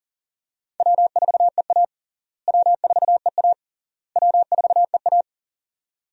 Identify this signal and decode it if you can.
Morse code, heard as a keyed tone, about 37 wpm, 700 Hz. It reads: W4EA W4EA W4EA
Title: USA Call Signs - 4 Characters - Repeated 3x - 40wpm